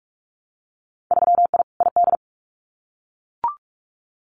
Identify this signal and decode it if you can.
Morse code, heard as a keyed tone, about 44 wpm, 700 Hz. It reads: 3I ID
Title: Winter Field Day Exchanges 45wpm